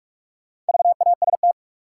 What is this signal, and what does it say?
Morse code, heard as a keyed tone, about 45 wpm, 700 Hz. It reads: VAST
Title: Sentences from Top 1300 Words - New Word Drill - 45wpm